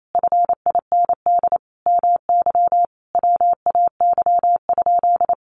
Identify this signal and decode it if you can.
Morse code, heard as a keyed tone, about 28 wpm, 700 Hz. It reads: FIND MY WAY?
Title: Sentences from Top 200 Words 28wpm